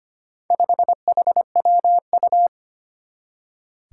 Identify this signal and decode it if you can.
Morse code, heard as a keyed tone, about 25 wpm, 700 Hz. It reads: 5HWU
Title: Four Challenging Letter-Number Combinations 25wpm